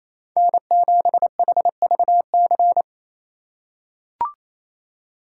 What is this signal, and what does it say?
Morse code, heard as a keyed tone, about 28 wpm, 700 Hz. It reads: N7HVC
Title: USA Call Signs - 5 Characters - 4x Character Spacing - 28wpm